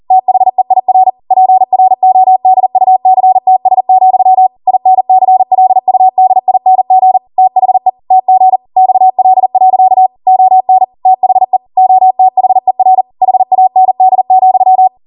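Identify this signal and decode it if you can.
Morse code, heard as a keyed tone, about 40 wpm, 750 Hz. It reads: THEIR PRODUCTS, INCLUDING THE TG XL. ON THE OTHER HAND,
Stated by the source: Code practice at 40 wpm.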